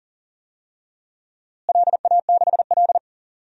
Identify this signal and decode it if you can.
Morse code, heard as a keyed tone, about 40 wpm, 700 Hz. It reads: LA6L
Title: DXCC-100 Call Signs - 4 Characters - Repeated 3x - 40wpm